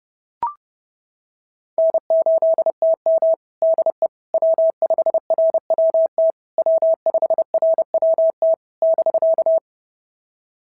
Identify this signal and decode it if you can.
Morse code, heard as a keyed tone, about 28 wpm, 650 Hz. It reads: N8TM DE W5RWT W5RWT <BK>
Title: SOTA QSO Practice 30wpm